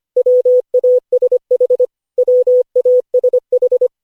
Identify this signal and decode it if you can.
Morse code, heard as a keyed tone, about 25 wpm, 500 Hz. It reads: WASH WASH
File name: cw / Session5 cw training